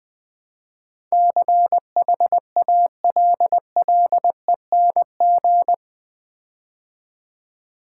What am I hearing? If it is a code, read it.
Morse code, heard as a keyed tone, about 20 wpm, 700 Hz. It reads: CHALLENG
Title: Build A Word - 4 Letters At A Time - 11-Letter Words - 20wpm